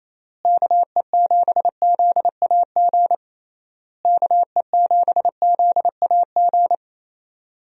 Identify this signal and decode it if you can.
Morse code, heard as a keyed tone, about 27 wpm, 700 Hz. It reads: KE7ZAG KE7ZAG
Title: USA Call Signs - 6 Characters - Repeated 3x - 28wpm